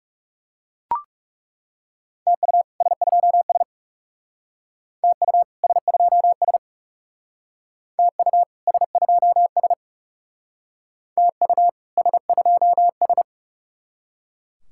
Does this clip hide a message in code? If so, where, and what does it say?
Yes: Morse code from 2.3 to 13.2 s, heard as a keyed tone, about 40 wpm, 700 Hz. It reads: TU S2S TU S2S TU S2S TU S2S